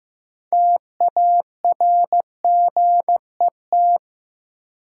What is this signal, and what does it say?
Morse code, heard as a keyed tone, about 15 wpm, 700 Hz. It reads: TARGET
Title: Sentences from Top 1200 Words - New Word Drill Alt - 15wpm